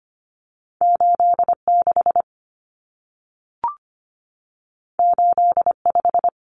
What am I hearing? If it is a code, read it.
Morse code, heard as a keyed tone, about 25 wpm, 700 Hz. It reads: 86 85
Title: Numbers - 2-Digit 25wpm